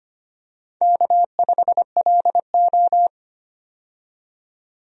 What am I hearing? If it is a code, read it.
Morse code, heard as a keyed tone, about 25 wpm, 700 Hz. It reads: K5LO
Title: USA Call Signs - 4 Characters - 25wpm